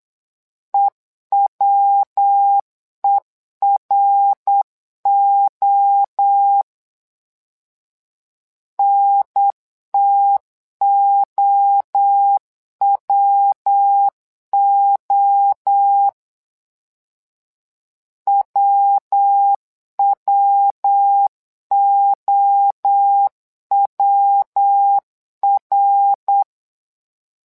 Morse code, heard as a keyed tone, about 9 wpm, 800 Hz. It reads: EWERO NTOWO WWOWR
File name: LEZIONI / 3 MWRO / 3 MWRO 8wpm 2606